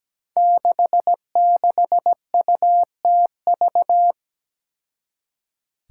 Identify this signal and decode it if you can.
Morse code, heard as a keyed tone, about 17 wpm, 700 Hz. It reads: 66UTV
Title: Five Challenging Letter-Number Combinations 17wpm